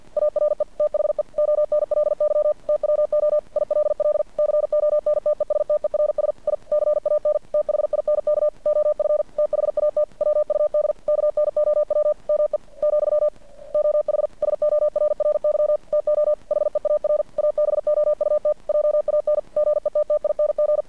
Morse code, heard as a keyed tone, about 50 wpm, 600 Hz. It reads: ARE THE ONLY TWO ILL CONTESTERS I CAN THINK OF THAT WUD KNOW ME , OH SORRY TO HEAR ABOUT JAN GETTING
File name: archived / CRSnet